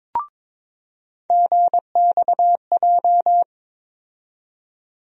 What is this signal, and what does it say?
Morse code, heard as a keyed tone, about 22 wpm, 700 Hz. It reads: GXJ
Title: USA Call Sign Postfixes - 3 Letters 22wpm